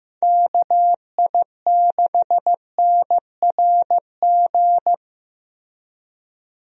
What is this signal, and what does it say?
Morse code, heard as a keyed tone, about 15 wpm, 700 Hz. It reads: KI6NRG
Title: USA Call Signs - 6 Characters - Repeated 3x - 15wpm